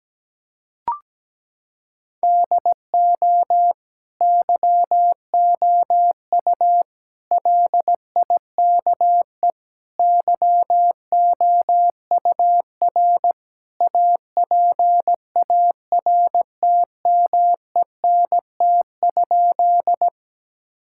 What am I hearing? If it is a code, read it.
Morse code, heard as a keyed tone, about 17 wpm, 700 Hz. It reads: DO YOU LIKE YOUR APARTMENT?
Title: Sentences from Top 1000 Words - Review Part 6 - 17wpm